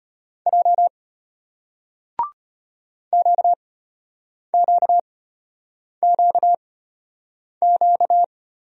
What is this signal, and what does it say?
Morse code, heard as a keyed tone, about 35 wpm, spaced as for 8 wpm, 700 Hz. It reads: J QQQQ